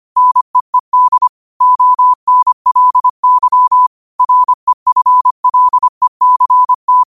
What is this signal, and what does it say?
Morse code, heard as a keyed tone, about 25 wpm, 1000 Hz. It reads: NEED ONLY REFLECT
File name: qotdfiles / 25wpm